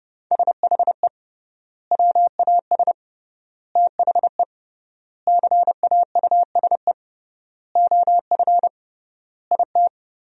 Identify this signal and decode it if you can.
Morse code, heard as a keyed tone, about 32 wpm, 700 Hz. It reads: SHE WAS THE CAUSE OF IT
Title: 3x Word-Spacing - Sentences from Top 200 Words - 30wpm